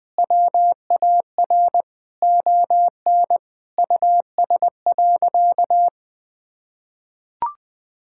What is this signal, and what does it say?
Morse code, heard as a keyed tone, about 20 wpm, 700 Hz. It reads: WAR ON US.
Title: Sentences from Top 1000 Words - Review Part 6 - 20wpm